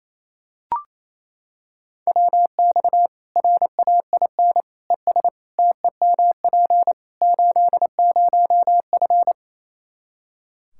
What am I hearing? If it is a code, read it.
Morse code, heard as a keyed tone, about 28 wpm, 700 Hz. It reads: WX RAIN ES TEMP 80F
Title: K1USN SST QSO Practice 28wpm